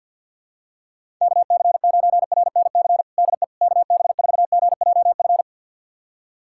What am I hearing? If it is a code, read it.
Morse code, heard as a keyed tone, about 47 wpm, 700 Hz. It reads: KC9RNC DE KB4GJF
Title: USA Callsign DE Callsign 50wpm